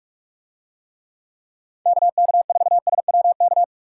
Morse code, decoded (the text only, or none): KK4SWK